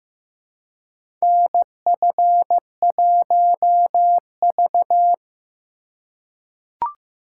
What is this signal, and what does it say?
Morse code, heard as a keyed tone, about 15 wpm, 700 Hz. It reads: NF1V
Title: USA Call Signs - 4 Characters - 4x Character Spacing - 15wpm